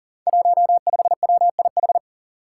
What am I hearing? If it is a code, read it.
Morse code, heard as a keyed tone, about 40 wpm, 700 Hz. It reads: 15WIH
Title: Five Challenging Letter-Number Combinations 40wpm